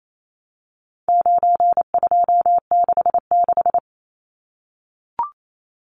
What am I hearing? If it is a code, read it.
Morse code, heard as a keyed tone, about 28 wpm, 700 Hz. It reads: 9266
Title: Numbers - 4-Digit 28wpm